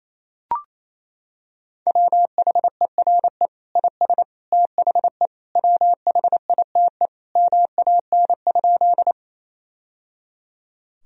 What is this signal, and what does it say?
Morse code, heard as a keyed tone, about 27 wpm, 700 Hz. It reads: WHERE IS THE WHITE MAN?
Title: Sentences from Top 1000 Words - Review Part 1 - 28wpm